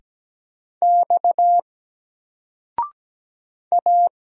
Morse code, heard as a keyed tone, about 17 wpm, 700 Hz. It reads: X A
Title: Single Character (Letter:Number:Punctuation:Prosign) 17wpm